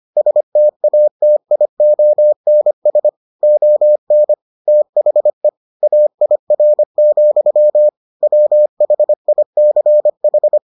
Morse code, heard as a keyed tone, about 25 wpm, 600 Hz. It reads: STATIONS ON THE AIR, WHICH